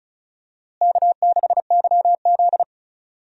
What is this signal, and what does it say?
Morse code, heard as a keyed tone, about 36 wpm, 700 Hz. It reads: K6YZ
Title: USA Call Signs - 4 Characters - Repeated 3x - 35wpm